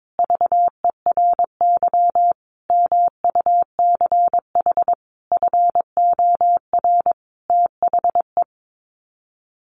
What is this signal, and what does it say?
Morse code, heard as a keyed tone, about 22 wpm, 700 Hz. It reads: VERY MUCH FOR THE
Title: Sets of 5 Words Encore2 22wpm